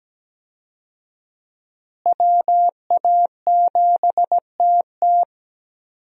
Morse code, heard as a keyed tone, about 17 wpm, 700 Hz. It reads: WA7TT
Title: USA Call Signs - 5 Characters - Repeated 3x - 17wpm